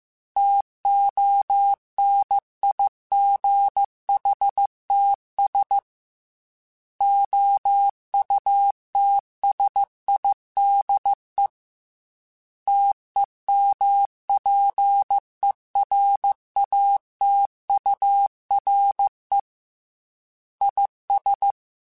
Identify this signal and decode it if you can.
Morse code, heard as a keyed tone, about 15 wpm, 800 Hz. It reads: TONIGHTS OUTSIDE TEMPERATURE IS
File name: Weather Sets / Weather Sets-mp3s / Weather 4 17 @13